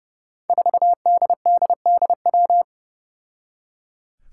Morse code, heard as a keyed tone, about 31 wpm, 700 Hz. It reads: 4DDDW
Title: Five Challenging Letter-Number Combinations 30wpm